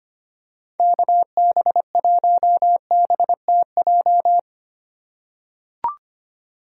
Morse code, heard as a keyed tone, about 25 wpm, 700 Hz. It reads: KB1BTJ
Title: USA Call Signs - 6 Characters - 2x Character Spacing - 25wpm